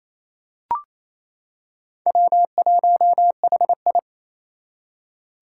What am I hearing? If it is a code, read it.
Morse code, heard as a keyed tone, about 28 wpm, 700 Hz. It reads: W1HI
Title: Four Challenging Letter-Number Combinations 28wpm